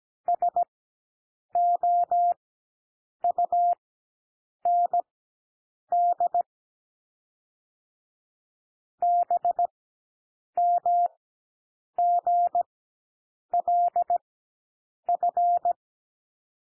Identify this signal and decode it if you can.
Morse code, heard as a keyed tone, about 17 wpm, spaced as for 7 wpm, 700 Hz. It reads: SOUND BMGLF
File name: SOUND LETTERS 17 wpm